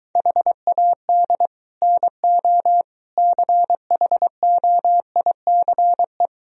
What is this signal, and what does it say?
Morse code, heard as a keyed tone, about 23 wpm, 700 Hz. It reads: HAD NO CHOICE